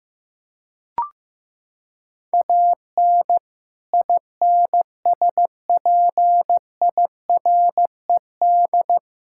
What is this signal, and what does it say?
Morse code, heard as a keyed tone, about 15 wpm, 700 Hz. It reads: AN INSPIRED
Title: Pangram Challenge v2 - 15wpm